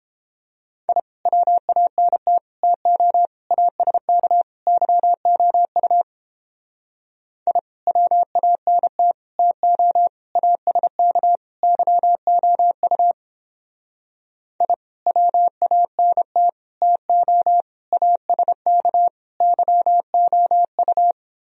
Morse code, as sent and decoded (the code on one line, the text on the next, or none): .. / .-- .- -. - / - --- / .- ... -.- / -.-- --- ..- / .. / .-- .- -. - / - --- / .- ... -.- / -.-- --- ..- / .. / .-- .- -. - / - --- / .- ... -.- / -.-- --- ..-
I WANT TO ASK YOU I WANT TO ASK YOU I WANT TO ASK YOU